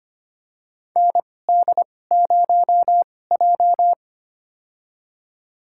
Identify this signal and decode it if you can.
Morse code, heard as a keyed tone, about 25 wpm, spaced as for 18 wpm, 700 Hz. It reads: ND0J
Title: USA Call Signs - 4 Characters - 2x Character Spacing - 25wpm